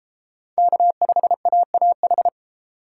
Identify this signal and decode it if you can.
Morse code, heard as a keyed tone, about 29 wpm, 700 Hz. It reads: K5AAH